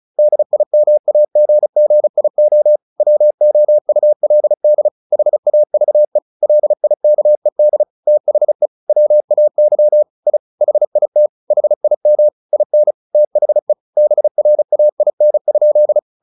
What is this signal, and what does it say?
Morse code, heard as a keyed tone, about 34 wpm, 600 Hz. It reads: DIMAGGIO WOULD HAVE LIKED THE WAY I HIT HIM IN THE BRAIN?
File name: the old man and the sea 2 35wpm